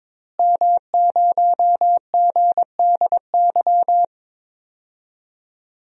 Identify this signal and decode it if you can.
Morse code, heard as a keyed tone, about 22 wpm, 700 Hz. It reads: M0GDY
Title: DXCC-100 Call Signs - 5 Characters - Repeated 3x - 22wpm